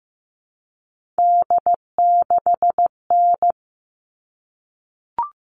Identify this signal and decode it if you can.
Morse code, heard as a keyed tone, about 15 wpm, 700 Hz. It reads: D6N
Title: All Cut Numbers - 3-Digit 15wpm